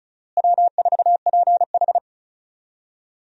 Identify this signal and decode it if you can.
Morse code, heard as a keyed tone, about 36 wpm, 700 Hz. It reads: W4PH
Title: USA Call Signs - 4 Characters - Repeated 2x - 35wpm